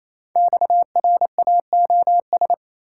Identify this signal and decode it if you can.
Morse code, heard as a keyed tone, about 27 wpm, 700 Hz. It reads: XRAOS
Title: Code Groups - 5 Letters - 28wpm